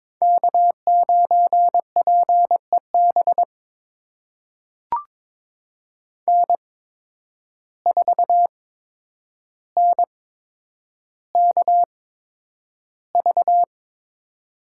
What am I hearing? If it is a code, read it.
Morse code, heard as a keyed tone, about 23 wpm, spaced as for 10 wpm, 700 Hz. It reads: K9PEB N4NKV
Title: USA Call Signs - 5 Characters - 8x Character Spacing - 22wpm